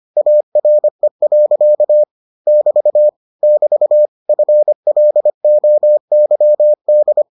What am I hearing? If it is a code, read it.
Morse code, heard as a keyed tone, about 25 wpm, 600 Hz. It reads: ARE. <BT> <BT> FLOYD